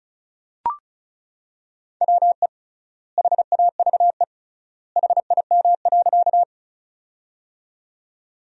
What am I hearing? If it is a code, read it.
Morse code, heard as a keyed tone, about 36 wpm, 700 Hz. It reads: WE HAVE HIM.
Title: 3x Word-Spacing - Sentences from Top 100 Words - 35wpm